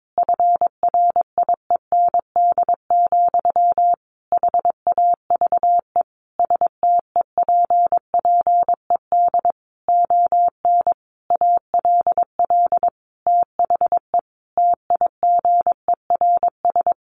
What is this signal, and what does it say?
Morse code, heard as a keyed tone, about 22 wpm, 700 Hz. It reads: FRIEND, HAVE STEPPED ON ALL THE TIGERS
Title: A Sound Of Thunder 22wpm